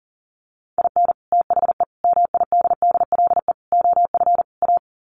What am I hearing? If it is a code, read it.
Morse code, heard as a keyed tone, about 40 wpm, 700 Hz. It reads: IN THE MIDDLE OF A
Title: Sets of 5 Words 40wpm